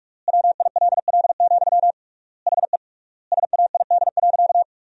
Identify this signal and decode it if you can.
Morse code, heard as a keyed tone, about 47 wpm, 700 Hz. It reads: WILL, HE SAID.
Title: 3x Word-Spacing - Sentences from Top 200 Words - 45wpm